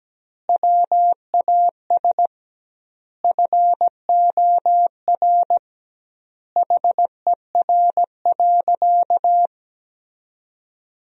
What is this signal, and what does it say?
Morse code, heard as a keyed tone, about 17 wpm, 700 Hz. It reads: WAS FOR HER.
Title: 2x Word-Spacing - Sentences from Top 400 Words - 17wpm